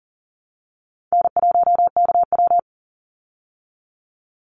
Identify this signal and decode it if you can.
Morse code, heard as a keyed tone, about 40 wpm, 700 Hz. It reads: N1KW
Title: Easy Call Signs Repeated 2x - 40wpm